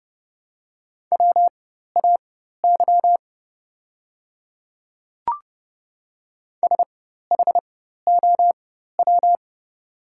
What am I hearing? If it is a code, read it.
Morse code, heard as a keyed tone, about 30 wpm, spaced as for 14 wpm, 700 Hz. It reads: WAY SHOW
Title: 4x Character Spacing - Top 200 Words - 30wpm